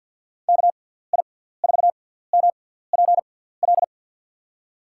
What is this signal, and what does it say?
Morse code, heard as a keyed tone, about 52 wpm, spaced as for 17 wpm, 700 Hz. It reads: KI4MPL
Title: USA Call Signs - 6 Characters - 6x Character Spacing - 50wpm